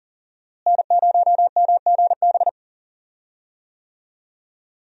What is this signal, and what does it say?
Morse code, heard as a keyed tone, about 40 wpm, 700 Hz. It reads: N0MGB
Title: USA Call Signs - 5 Characters - Repeated 3x - 40wpm